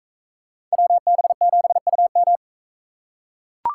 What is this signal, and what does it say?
Morse code, heard as a keyed tone, about 39 wpm, 700 Hz. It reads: WB7UM